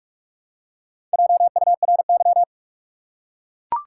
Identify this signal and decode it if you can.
Morse code, heard as a keyed tone, about 44 wpm, 700 Hz. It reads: JURY
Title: Four Letter Words - 45wpm